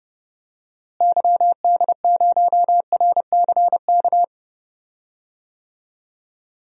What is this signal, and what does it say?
Morse code, heard as a keyed tone, about 30 wpm, 700 Hz. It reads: YD0RCK
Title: DXCC-100 Call Signs - 6 Characters - Repeated 3x - 30wpm